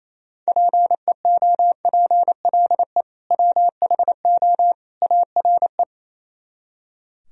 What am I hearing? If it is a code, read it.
Morse code, heard as a keyed tone, about 34 wpm, 700 Hz. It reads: PEOPLE WHO ARE
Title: Speed-Racing - Sets of 3 Words 28wpm